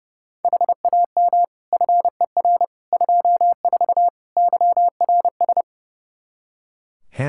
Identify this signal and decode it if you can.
Morse code, heard as a keyed tone, about 30 wpm, 700 Hz. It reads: HAM FER 24 YRS
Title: Practice QSO - Exchange 3 - More Info - 30wpm